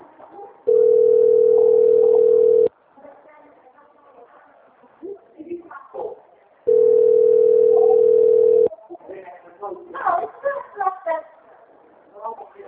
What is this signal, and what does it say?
Ringback tone (North America), 440 + 480 Hz, with speech.
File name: rg-1600-unknown- - - .46686